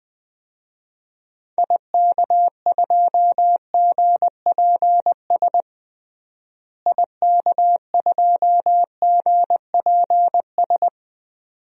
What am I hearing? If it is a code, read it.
Morse code, heard as a keyed tone, about 20 wpm, 700 Hz. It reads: IK2GPS IK2GPS
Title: DXCC-100 Call Signs - 6 Characters - Repeated 2x - 20wpm